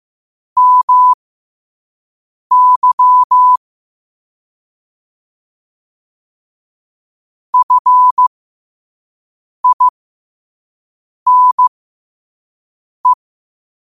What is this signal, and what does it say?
Morse code, heard as a keyed tone, about 15 wpm, spaced as for 5 wpm, 1000 Hz. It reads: MY FINE
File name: qotdfiles / 5wpm